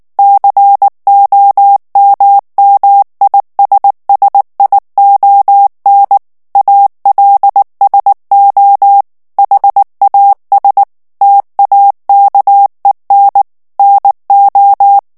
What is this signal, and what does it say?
Morse code, heard as a keyed tone, about 19 wpm, 800 Hz. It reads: COMMISSION ALSO HAS TAKEN NO
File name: ANGLAIS texte clair 3 vitesse 18 mots minute